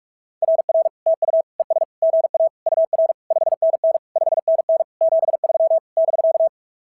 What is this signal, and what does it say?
Morse code, heard as a keyed tone, about 46 wpm, 650 Hz. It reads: RR TU ES GA UR 5NN 5NN 73 <BK>
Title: SOTA QSO Practice 45wpm